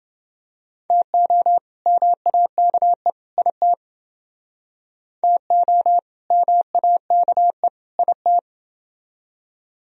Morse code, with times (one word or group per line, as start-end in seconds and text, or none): 0.9-1.6 TO
1.9-3.1 MAKE
3.4-3.7 IT
5.2-6.0 TO
6.3-7.7 MAKE
8.0-8.4 IT